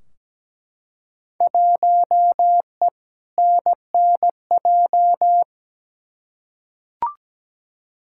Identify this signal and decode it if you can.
Morse code, heard as a keyed tone, about 17 wpm, 700 Hz. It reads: 1E NNJ
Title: Field Day Exchanges - 17wpm